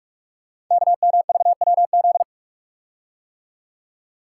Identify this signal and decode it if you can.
Morse code, heard as a keyed tone, about 44 wpm, 690 Hz. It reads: KMVWZ
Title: Code Groups - 5 Letters - 45wpm